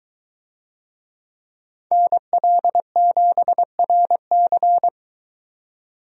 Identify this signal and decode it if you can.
Morse code, heard as a keyed tone, about 21 wpm, 700 Hz. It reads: NL7RC